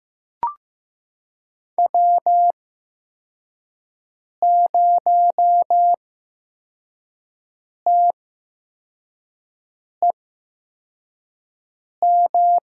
Morse code, heard as a keyed tone, about 15 wpm, spaced as for 4 wpm, 700 Hz. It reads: W0TEM
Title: USA Call Signs - 8x Character Spacing - 15wpm